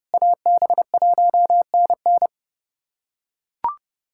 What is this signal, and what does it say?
Morse code, heard as a keyed tone, about 30 wpm, 700 Hz. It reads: AB1NN
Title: USA Callsign DE Callsign - 5 Characters - 30wpm